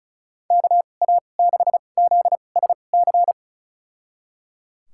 Morse code, heard as a keyed tone, about 36 wpm, spaced as for 25 wpm, 700 Hz. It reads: KA6ZSC
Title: USA Call Signs - 6 Characters - 2x Character Spacing - 35wpm